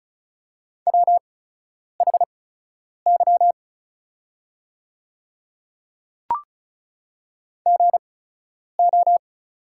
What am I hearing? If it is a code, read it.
Morse code, heard as a keyed tone, about 34 wpm, spaced as for 10 wpm, 690 Hz. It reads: WHY GO
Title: 8x Character Spacing - Top 200 Words - 35wpm